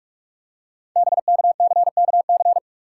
Morse code, heard as a keyed tone, about 44 wpm, 700 Hz. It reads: BKCKC
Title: Code Groups - 5 Letters - 45wpm